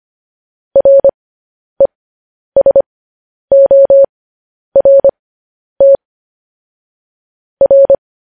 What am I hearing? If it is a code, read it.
Morse code, heard as a keyed tone, about 25 wpm, spaced as for 10 wpm, 550 Hz. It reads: RESORT R